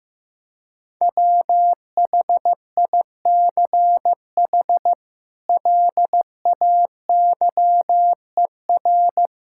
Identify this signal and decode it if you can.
Morse code, heard as a keyed tone, about 16 wpm, 700 Hz. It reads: WHICH LAYER
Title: Sentences from Top 1600 Words 15wpm